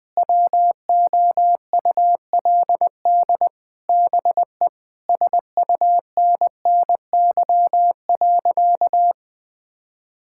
Morse code, heard as a keyed tone, about 20 wpm, 700 Hz. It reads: WOULD BE SUNNY.
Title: Sentences with 4-Word Phrases - Spoken Beginning - 20wpm